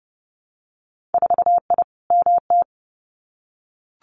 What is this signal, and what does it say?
Morse code, heard as a keyed tone, about 30 wpm, 700 Hz. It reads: 4I MT
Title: Winter Field Day Exchanges Encore 30wpm